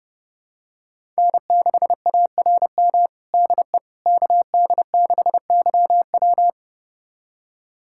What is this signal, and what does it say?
Morse code, heard as a keyed tone, about 30 wpm, 700 Hz. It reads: N6ARM DE KD6YW
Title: USA Callsign DE Callsign - 5 Characters - 30wpm